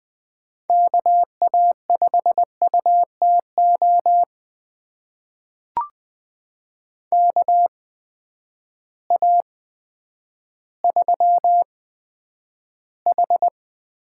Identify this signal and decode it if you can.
Morse code, heard as a keyed tone, about 20 wpm, spaced as for 10 wpm, 700 Hz. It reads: KA5UTO KA3H
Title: USA Call Signs - 8x Character Spacing - 20wpm